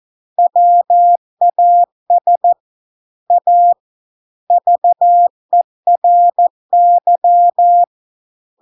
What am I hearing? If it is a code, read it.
Morse code, heard as a keyed tone, about 14 wpm, 700 Hz. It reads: WAS A VERY